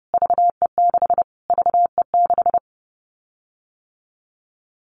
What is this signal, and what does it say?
Morse code, heard as a keyed tone, about 30 wpm, 700 Hz. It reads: VE6 VE6
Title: Callsign Prefixes for Top 25 DXCC - Repeated 3x - 30wpm